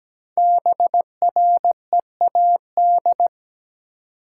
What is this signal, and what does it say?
Morse code, heard as a keyed tone, about 17 wpm, 700 Hz. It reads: BREAD
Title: Sentences from Top 1600 Words - New Word Drill Alt - 17wpm